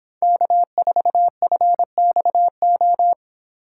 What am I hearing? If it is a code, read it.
Morse code, heard as a keyed tone, about 26 wpm, 700 Hz. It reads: K4FXO
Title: Speed Racing - USA Call Signs 22wpm